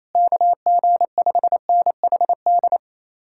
Morse code, heard as a keyed tone, about 37 wpm, 700 Hz. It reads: KG5NHD